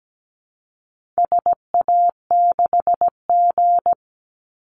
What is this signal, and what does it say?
Morse code, heard as a keyed tone, about 17 wpm, 700 Hz. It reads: SA6G
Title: Call Signs Repeated 3x - 17wpm